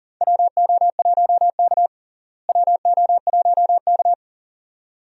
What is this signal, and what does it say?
Morse code, heard as a keyed tone, about 37 wpm, 700 Hz. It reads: WO1K WO1K
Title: USA Call Signs - 4 Characters - Repeated 2x - 40wpm